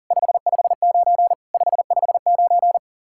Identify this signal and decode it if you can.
Morse code, heard as a keyed tone, about 47 wpm, 700 Hz. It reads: 559 559
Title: Speed-Racing Common QSO Phrases - 40wpm